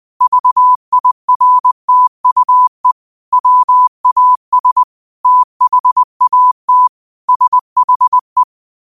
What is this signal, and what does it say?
Morse code, heard as a keyed tone, about 20 wpm, 1000 Hz. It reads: VIRTUE WAS THAT SHE
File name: 20wpm